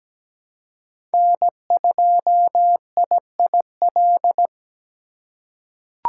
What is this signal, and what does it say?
Morse code, heard as a keyed tone, about 17 wpm, 700 Hz. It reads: N2IIL
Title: USA Call Signs - 5 Characters - 8x Character Spacing - 17wpm